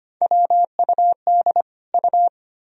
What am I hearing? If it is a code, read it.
Morse code, heard as a keyed tone, about 25 wpm, 700 Hz. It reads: WUD U
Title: Sentences from Top 100 Words and Abbreviations 25wpm